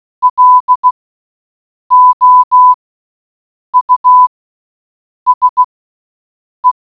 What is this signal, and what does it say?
Morse code, heard as a keyed tone, about 16 wpm, spaced as for 7 wpm, 1000 Hz. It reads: LOUSE